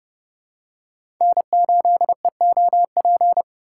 Morse code, heard as a keyed tone, about 30 wpm, 700 Hz. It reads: N8EOP
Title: USA Call Signs - 2x Character Spacing - 30wpm